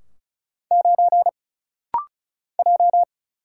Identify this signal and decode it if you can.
Morse code, heard as a keyed tone, about 34 wpm, 700 Hz. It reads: 9 J
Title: Single Letter-Number - Rapid-Fire 35wpm